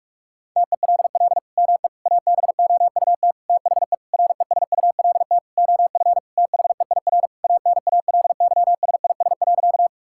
Morse code, heard as a keyed tone, about 46 wpm, 700 Hz. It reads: TELL ME ABOUT THE RESULT OF THEIR ANALYSIS.
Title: Sentences from Top 1000 Words - Review Part 5 - 45wpm